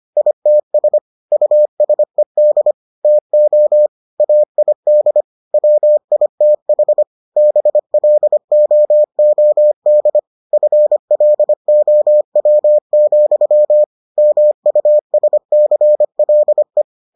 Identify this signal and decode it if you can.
Morse code, heard as a keyed tone, about 25 wpm, 600 Hz. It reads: ITS USED TO AID WITH BLOOD FLOW, MUSCLE